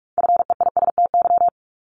Morse code, heard as a keyed tone, about 46 wpm, 700 Hz. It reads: FEISTY
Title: Multi-letter Phonemes Repeated 3x 45wpm